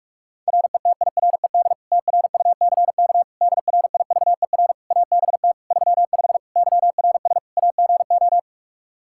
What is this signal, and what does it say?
Morse code, heard as a keyed tone, about 46 wpm, 700 Hz. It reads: RETIRED TRUCK DRIVER ABT 35 YRS AGO
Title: Practice QSO - Exchange 3 - More Info - 45wpm